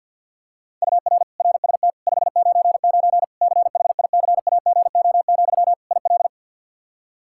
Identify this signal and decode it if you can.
Morse code, heard as a keyed tone, about 52 wpm, 700 Hz. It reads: UR RST 599 CHICAGO, IL
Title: Practice QSO - Exchange 1 - Beginning - 50wpm